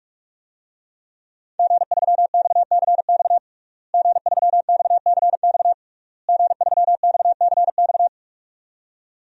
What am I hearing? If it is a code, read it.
Morse code, heard as a keyed tone, about 46 wpm, 700 Hz. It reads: G3XCX G3XCX G3XCX
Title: DXCC-100 Call Signs - 5 Characters - Repeated 3x - 45wpm